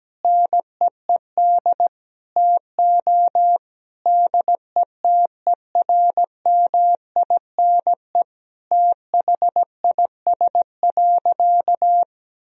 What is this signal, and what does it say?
Morse code, heard as a keyed tone, about 17 wpm, 700 Hz. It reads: NEED TO DETERMINE THIS.
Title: Sentences from Top 1000 Words - Review Part 4 - 17wpm